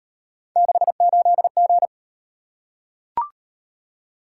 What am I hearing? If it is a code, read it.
Morse code, heard as a keyed tone, about 35 wpm, 700 Hz. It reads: 68G